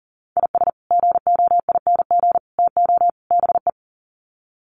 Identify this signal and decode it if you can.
Morse code, heard as a keyed tone, about 39 wpm, 700 Hz. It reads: IS GOING TO BE
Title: Sets of 4 Words 40wpm